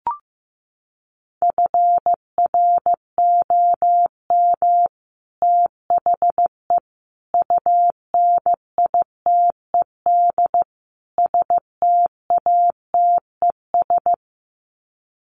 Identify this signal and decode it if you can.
Morse code, heard as a keyed tone, about 15 wpm, 700 Hz. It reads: FROM THE UNITED STATES
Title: Sets of 4 Words 15wpm